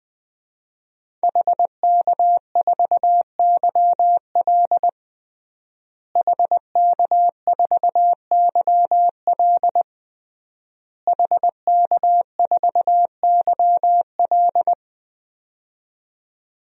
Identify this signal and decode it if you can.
Morse code, heard as a keyed tone, about 20 wpm, 700 Hz. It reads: HK4YL HK4YL HK4YL
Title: DXCC-100 Call Signs - 5 Characters - Repeated 3x - 20wpm